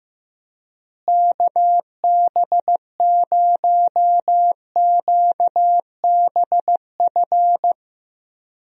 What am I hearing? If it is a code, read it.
Morse code, heard as a keyed tone, about 15 wpm, 700 Hz. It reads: KB0QBF
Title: USA Call Signs - 6 Characters - Repeated 2x - 15wpm